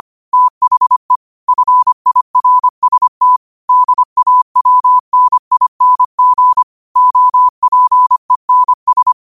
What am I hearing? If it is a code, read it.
Morse code, heard as a keyed tone, about 25 wpm, 1000 Hz. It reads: THE FIRST DAWNING OPENS
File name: qotdfiles / 25wpm